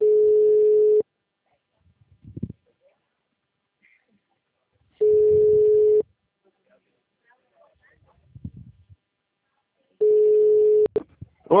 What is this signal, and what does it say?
Ringback tone (Europe, 425 Hz), with speech.